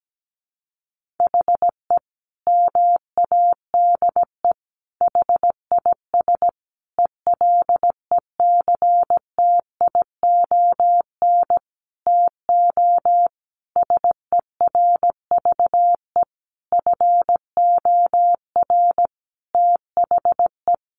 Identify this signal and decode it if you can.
Morse code, heard as a keyed tone, about 17 wpm, 700 Hz. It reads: HE MADE HIS ELECTION TO SERVE FOR THE
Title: Sentences from Top 1500 Words 17wpm